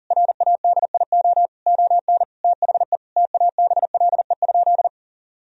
Morse code, heard as a keyed tone, about 40 wpm, 700 Hz. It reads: RADIO ON THE TABLE?
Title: Sentences from Top 1000 Words - Review Part 3 - 40wpm